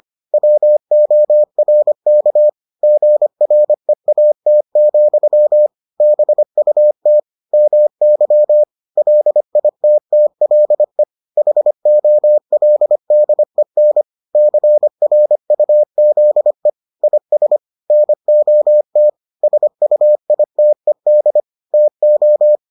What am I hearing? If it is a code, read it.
Morse code, heard as a keyed tone, about 25 wpm, 600 Hz. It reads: WORK GREAT, BUT MY LITTLE HOLDEN CRUZE IS NOT SUITED TO